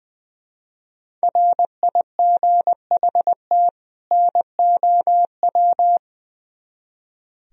Morse code, heard as a keyed tone, about 22 wpm, 700 Hz. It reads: RIGHT NOW
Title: Speed-Racing - Sets of 2 Words 20wpm